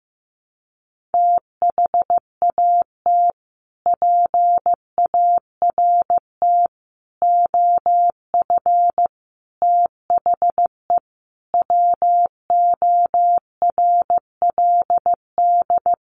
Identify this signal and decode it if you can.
Morse code, heard as a keyed tone, about 14 wpm, 700 Hz. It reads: THAT PART OF THE WORLD
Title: Sets of 5 Words Encore2 15wpm